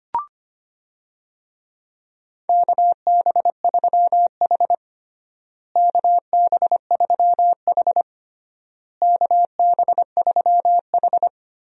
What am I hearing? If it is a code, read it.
Morse code, heard as a keyed tone, about 25 wpm, 700 Hz. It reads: KB3H KB3H KB3H
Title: USA Call Signs - 4 Characters - Repeated 3x - 25wpm